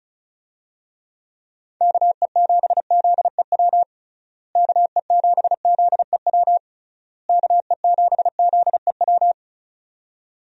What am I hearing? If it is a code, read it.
Morse code, heard as a keyed tone, about 34 wpm, 700 Hz. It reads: KE7ZEW KE7ZEW KE7ZEW
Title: USA Call Signs - 6 Characters - Repeated 3x - 35wpm